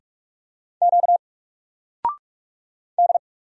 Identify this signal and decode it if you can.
Morse code, heard as a keyed tone, about 46 wpm, 700 Hz. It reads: Q D
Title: Single Letter-Number 45wpm